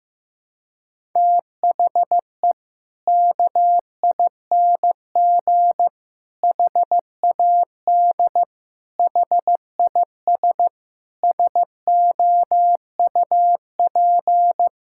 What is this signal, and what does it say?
Morse code, heard as a keyed tone, about 15 wpm, 700 Hz. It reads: THE KING HAD HIS SOUP
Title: Sentences from Top 1000 Words - Review Part 6 - 15wpm